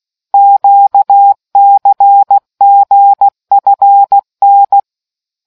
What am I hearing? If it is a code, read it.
Morse code, heard as a keyed tone, about 16 wpm, 800 Hz. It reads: QCGFN